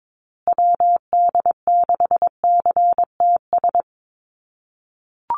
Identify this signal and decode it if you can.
Morse code, heard as a keyed tone, about 22 wpm, 700 Hz. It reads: WD6CTS
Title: USA Call Signs Repeated 3x 22wpm